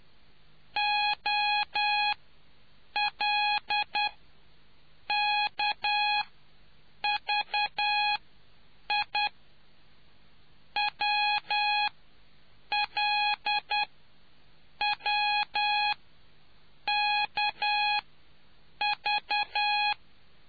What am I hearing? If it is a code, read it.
Morse code, heard as a keyed tone, about 10 wpm, spaced as for 7 wpm, 830 Hz. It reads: OL KVI WL W KV